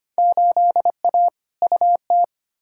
Morse code, heard as a keyed tone, about 25 wpm, 700 Hz. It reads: 8A UT
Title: Field Day Exchanges Encore - 25wpm